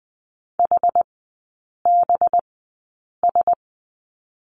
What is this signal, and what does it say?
Morse code, heard as a keyed tone, about 20 wpm, spaced as for 8 wpm, 690 Hz. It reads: HBS
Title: 212 Lesson 34e - Number 8 - Three Characters 20wpm